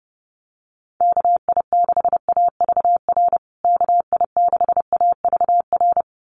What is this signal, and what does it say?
Morse code, heard as a keyed tone, about 30 wpm, 700 Hz. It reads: KI6AVR KI6AVR
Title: USA Call Signs Repeated 2x 30wpm